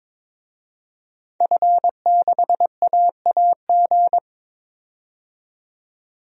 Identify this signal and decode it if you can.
Morse code, heard as a keyed tone, about 21 wpm, 700 Hz. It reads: F6AAG
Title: DXCC-100 Call Signs - 5 Characters - 22wpm